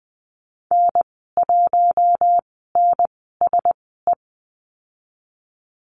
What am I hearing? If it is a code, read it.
Morse code, heard as a keyed tone, about 20 wpm, spaced as for 15 wpm, 700 Hz. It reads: N1NSE
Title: 042 Lesson 8g - Number 1 - Call Signs 20wpm